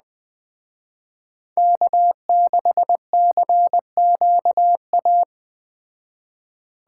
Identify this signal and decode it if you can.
Morse code, heard as a keyed tone, about 21 wpm, 700 Hz. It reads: K6CQA
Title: Speed Racing - USA 1x3 Call Signs 17wpm